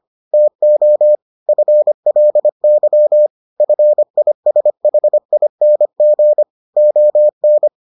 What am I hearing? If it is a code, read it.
Morse code, heard as a keyed tone, about 25 wpm, 600 Hz. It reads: TO FLY FISHING ON